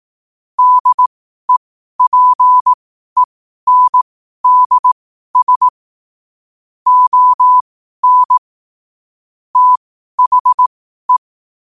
Morse code, heard as a keyed tone, about 18 wpm, spaced as for 12 wpm, 1000 Hz. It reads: DEPENDS ON THE
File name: qotdfiles / 12wpm